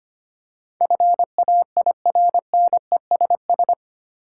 Morse code, heard as a keyed tone, about 25 wpm, 690 Hz. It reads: FAIRNESS
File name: 8 Letter Words - First 3 Letters Given - 25wpm